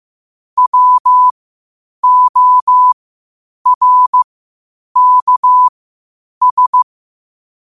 Morse code, heard as a keyed tone, about 15 wpm, spaced as for 8 wpm, 1000 Hz. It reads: WORKS
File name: qotdfiles / 8wpm